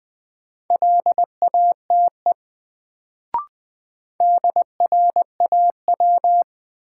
Morse code, heard as a keyed tone, about 20 wpm, 700 Hz. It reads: LATE DRAW
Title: Sentences from Top 300 Words - New Word Drill - 20wpm